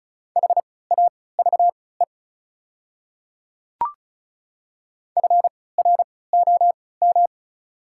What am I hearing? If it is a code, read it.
Morse code, heard as a keyed tone, about 34 wpm, spaced as for 20 wpm, 690 Hz. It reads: HAVE FROM
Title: 3x Character Spacing - Top 100 Words - 35wpm